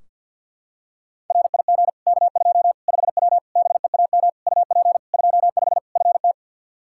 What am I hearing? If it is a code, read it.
Morse code, heard as a keyed tone, about 46 wpm, 700 Hz. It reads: RIG K2 5W BEAM UP 25 FT
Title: Practice QSO - Exchange 2 - Info - 50wpm